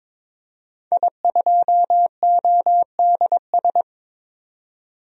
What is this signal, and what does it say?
Morse code, heard as a keyed tone, about 22 wpm, 700 Hz. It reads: I2ODS
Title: DXCC-100 Call Signs - 5 Characters - 22wpm